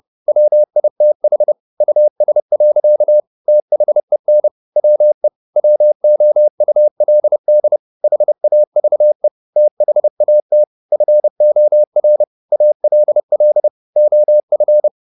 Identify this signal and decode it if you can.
Morse code, heard as a keyed tone, about 30 wpm, 600 Hz. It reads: WITH US. THEN WE WOULD HAVE THAT FOR ALL OF
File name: the old man and the sea / the old man and the sea 1 30wpm